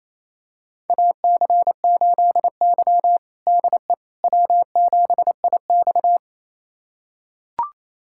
Morse code, heard as a keyed tone, about 28 wpm, 700 Hz. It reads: AC8Y DE W7IX
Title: USA Callsign DE Callsign - 4 Characters - 28wpm